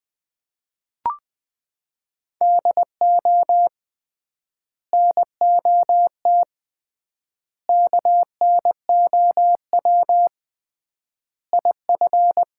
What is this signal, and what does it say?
Morse code, heard as a keyed tone, about 21 wpm, 700 Hz. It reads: DO NOT KNOW IF
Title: 3x Word-Spacing - Sets of 5 Words - 20wpm